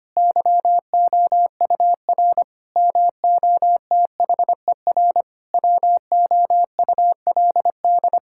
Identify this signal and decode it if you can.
Morse code, heard as a keyed tone, about 25 wpm, 700 Hz. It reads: YOUR MOTHER WOULD
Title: Sentences from Top 1600 Words 25wpm